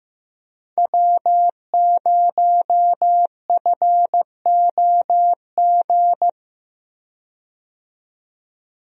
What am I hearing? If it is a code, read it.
Morse code, heard as a keyed tone, about 15 wpm, 700 Hz. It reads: W0FOG
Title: USA Call Signs - 5 Characters - 15wpm